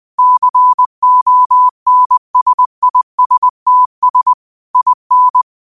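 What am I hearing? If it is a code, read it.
Morse code, heard as a keyed tone, about 20 wpm, 1000 Hz. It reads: CONSISTS IN
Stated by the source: Morse code quote of the day at 20 wpm.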